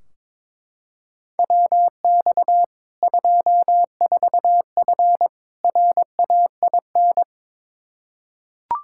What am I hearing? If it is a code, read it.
Morse code, heard as a keyed tone, about 22 wpm, 700 Hz. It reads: WX 24F RAIN
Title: Practice QSO - Exchange 2 - Info - 22wpm